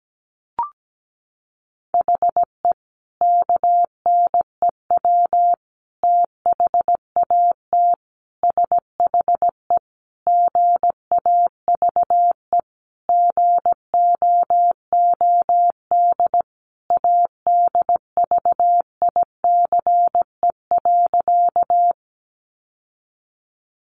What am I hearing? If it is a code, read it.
Morse code, heard as a keyed tone, about 17 wpm, 700 Hz. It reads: HE KNEW THAT SHE GAVE GOOD ADVICE.
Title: Sentences from Top 1100 Words 17wpm